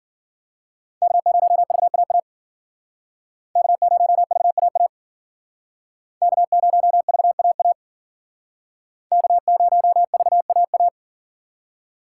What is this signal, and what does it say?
Morse code, heard as a keyed tone, about 52 wpm, 700 Hz. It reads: K0VAA K0VAA K0VAA K0VAA